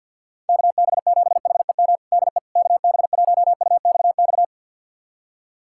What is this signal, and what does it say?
Morse code, heard as a keyed tone, about 52 wpm, 700 Hz. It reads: KB7HEM DE KB1UXX
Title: USA Callsign DE Callsign - 6 Characters - 50wpm